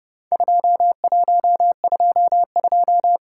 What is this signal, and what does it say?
Morse code, heard as a keyed tone, about 30 wpm, 700 Hz. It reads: 2122
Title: Worked All Europe DX Contest - QTC Exchange - 30wpm